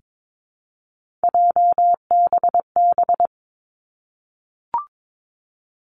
Morse code, heard as a keyed tone, about 22 wpm, 700 Hz. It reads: JBB
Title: Three Challenging Letter-Number Combinations 22wpm